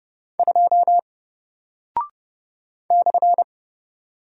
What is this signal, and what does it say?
Morse code, heard as a keyed tone, about 30 wpm, 700 Hz. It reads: 2 /
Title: Single Character (Letter:Number:Punctuation:Prosign) 30wpm